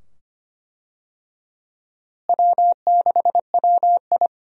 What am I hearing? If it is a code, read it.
Morse code, heard as a keyed tone, about 25 wpm, 700 Hz. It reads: W6WI
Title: USA Call Signs - 4 Characters - Repeated 2x - 25wpm